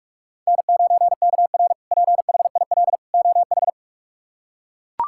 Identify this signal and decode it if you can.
Morse code, heard as a keyed tone, about 41 wpm, 700 Hz. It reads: N9KR PHIL OH
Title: K1USN SST QSO Practice - 45wpm